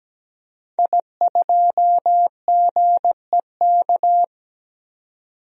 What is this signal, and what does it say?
Morse code, heard as a keyed tone, about 17 wpm, 700 Hz. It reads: I2GEK
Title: DXCC-100 Call Signs - 5 Characters - 17wpm